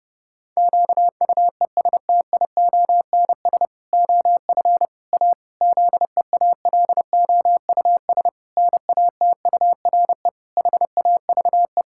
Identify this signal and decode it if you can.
Morse code, heard as a keyed tone, about 30 wpm, 700 Hz. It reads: QUESTIONS OF A ZEALOUS NATURE HAVE
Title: Pangram Challenge v2 - 30wpm